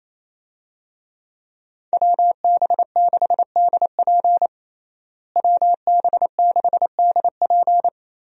Morse code, heard as a keyed tone, about 28 wpm, 700 Hz. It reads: WB6DP WB6DP
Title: USA Call Signs - 5 Characters - Repeated 2x - 28wpm